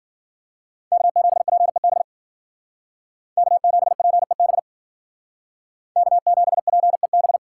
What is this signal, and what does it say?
Morse code, heard as a keyed tone, about 50 wpm, 700 Hz. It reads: K7PEB K7PEB K7PEB
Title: Speed Racing - USA Call Signs 40wpm